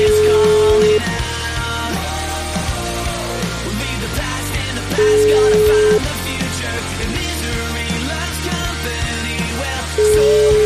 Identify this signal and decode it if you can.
Ringback tone (Europe, 425 Hz), with noise.